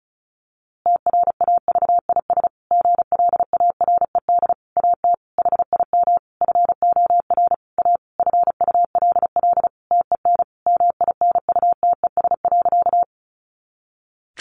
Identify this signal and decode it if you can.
Morse code, heard as a keyed tone, about 34 wpm, 700 Hz. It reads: TRAVIS GLARED AT HIM FOR A FULL TEN MINUTES.
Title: A Sound Of Thunder 35wpm